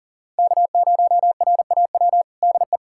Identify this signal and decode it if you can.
Morse code, heard as a keyed tone, about 40 wpm, 700 Hz. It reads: K0RAW DE
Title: USA Callsign DE Callsign - 5 Characters - 40wpm